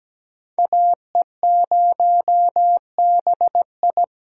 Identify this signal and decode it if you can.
Morse code, heard as a keyed tone, about 20 wpm, 700 Hz. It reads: AE0BI
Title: Speed Racing - USA 2x2 Call Signs 17wpm